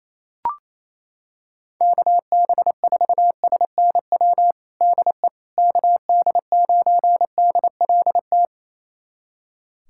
Morse code, heard as a keyed tone, about 28 wpm, 700 Hz. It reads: KB4SNW DE KD9DLT
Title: USA Callsign DE Callsign 28wpm